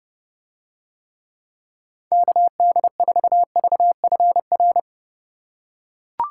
Morse code, heard as a keyed tone, about 29 wpm, 700 Hz. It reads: KD4VFR